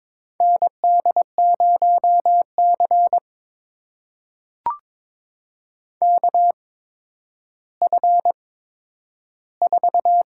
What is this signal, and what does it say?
Morse code, heard as a keyed tone, about 22 wpm, spaced as for 11 wpm, 700 Hz. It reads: ND0C KF4
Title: USA Call Signs - 4 Characters - 8x Character Spacing - 22wpm